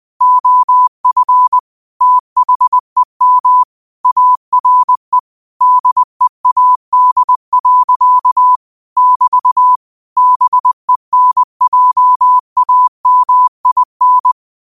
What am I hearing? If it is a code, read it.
Morse code, heard as a keyed tone, about 20 wpm, 1000 Hz. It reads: OF THEM ARE DEAD. <BT> BENJAMIN
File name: 20wpm